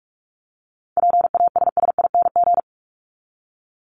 Morse code, heard as a keyed tone, about 44 wpm, 700 Hz. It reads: PASSING
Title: ING Suffix - Top 100 Words with Base Words 45wpm